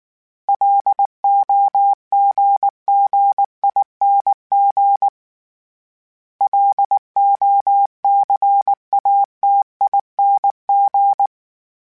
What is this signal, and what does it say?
Morse code, heard as a keyed tone, about 19 wpm, 800 Hz. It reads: LOGGING LOCATING